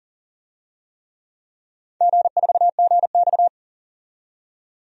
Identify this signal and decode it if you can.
Morse code, heard as a keyed tone, about 40 wpm, 700 Hz. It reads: G4GX
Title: DXCC-100 Call Signs - 4 Characters - 40wpm